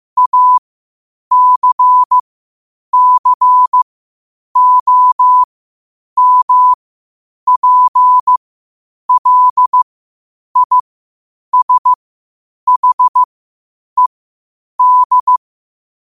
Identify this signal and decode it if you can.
Morse code, heard as a keyed tone, about 15 wpm, spaced as for 8 wpm, 1000 Hz. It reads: ACCOMPLISHED
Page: Quotes for Fri, 15 Aug 2025 in Morse Code at 8 words per minute.